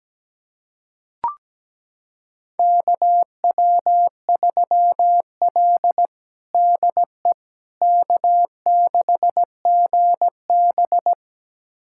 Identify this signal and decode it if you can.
Morse code, heard as a keyed tone, about 17 wpm, 700 Hz. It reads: KW3L DE K6GB
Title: USA Callsign DE Callsign - 4 Characters - 17wpm